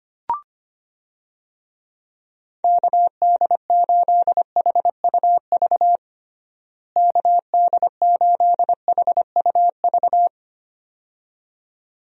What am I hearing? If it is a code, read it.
Morse code, heard as a keyed tone, about 25 wpm, 700 Hz. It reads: KD8HUV KD8HUV
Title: USA Call Signs - 6 Characters - Repeated 2x - 25wpm